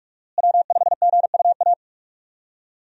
Morse code, heard as a keyed tone, about 42 wpm, 700 Hz. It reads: W5GUA
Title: USA Call Signs - 5 Characters - Repeated 2x - 45wpm